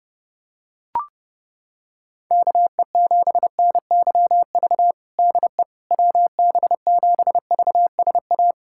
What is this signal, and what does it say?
Morse code, heard as a keyed tone, about 30 wpm, 700 Hz. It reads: KE7NYV DE WB7VSA
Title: USA Callsign DE Callsign 30wpm